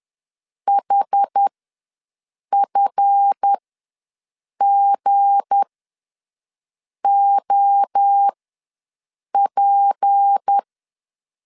Morse code, heard as a keyed tone, about 10 wpm, spaced as for 6 wpm, 800 Hz. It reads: HFGOP
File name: LEZIONI / 8 YJ45 / 8 YJ45 ws 10wpm 3843